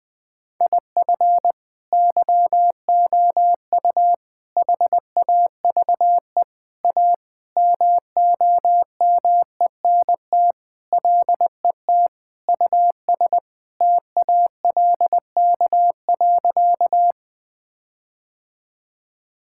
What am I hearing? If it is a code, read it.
Morse code, heard as a keyed tone, about 21 wpm, 700 Hz. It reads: IF YOU HAVE A MOMENT LET US TALK.
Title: Sentences with 4-Word Phrases - 20wpm